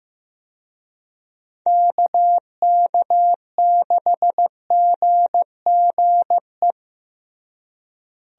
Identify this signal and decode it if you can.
Morse code, heard as a keyed tone, about 15 wpm, 700 Hz. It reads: KK6GGE
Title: USA Call Signs - 6 Characters - Repeated 2x - 15wpm